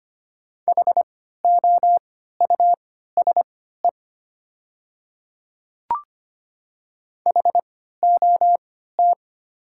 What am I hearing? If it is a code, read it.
Morse code, heard as a keyed tone, about 25 wpm, spaced as for 14 wpm, 700 Hz. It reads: HOUSE HOT
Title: 3x Character Spacing - Top 300 Words - 25wpm